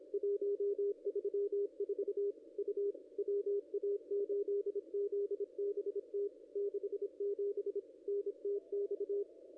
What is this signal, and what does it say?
Morse code, heard as a keyed tone, about 28 wpm, 400 Hz. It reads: 134 U WA8ZBT 67 NTX
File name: contests / ss-cw-k5tr-op-n5rz -0200